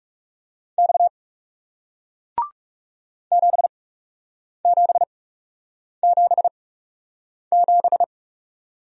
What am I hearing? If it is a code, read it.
Morse code, heard as a keyed tone, about 40 wpm, spaced as for 8 wpm, 700 Hz. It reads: X 7777